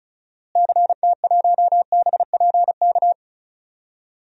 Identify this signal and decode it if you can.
Morse code, heard as a keyed tone, about 36 wpm, 700 Hz. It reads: CT1BPK
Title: DXCC-100 Call Signs - 6 Characters - Repeated 3x - 35wpm